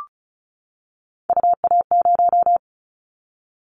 Morse code, heard as a keyed tone, about 36 wpm, 700 Hz. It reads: UA0
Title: Callsign Prefixes for Top 50 DXCC - 35wpm